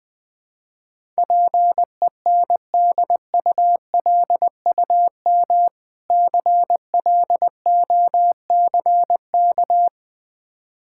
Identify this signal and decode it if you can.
Morse code, heard as a keyed tone, about 21 wpm, 700 Hz. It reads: PENDULUM CLOCK
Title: Difficult Words - Speed-Racing with Hints - 20wpm